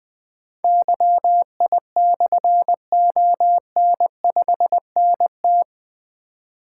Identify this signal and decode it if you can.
Morse code, heard as a keyed tone, about 20 wpm, 700 Hz. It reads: YI/ON5NT
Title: Difficult Call Signs 20wpm